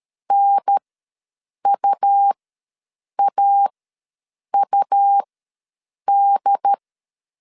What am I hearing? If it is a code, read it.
Morse code, heard as a keyed tone, about 13 wpm, spaced as for 7 wpm, 800 Hz. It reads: NUAUD
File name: LEZIONI / 6 PKX0 / 6 P ws 12wpm 3117